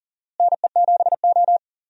Morse code, heard as a keyed tone, about 40 wpm, 700 Hz. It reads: NE7O
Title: USA Call Signs - 4 Characters - Repeated 2x - 40wpm